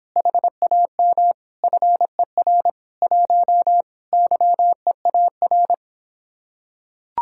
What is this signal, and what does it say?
Morse code, heard as a keyed tone, about 26 wpm, 700 Hz. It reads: HAM FER 1 YEAR